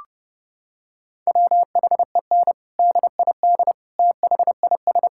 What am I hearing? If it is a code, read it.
Morse code, heard as a keyed tone, about 32 wpm, 700 Hz. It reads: WHEN DID THIS
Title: Sentences from Top 1000 Words - Review Part 6 - 30wpm